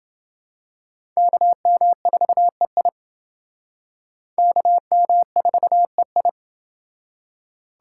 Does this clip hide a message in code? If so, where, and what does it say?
Yes: Morse code from 1.2 to 6.3 s, heard as a keyed tone, about 28 wpm, 700 Hz. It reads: KM4EI KM4EI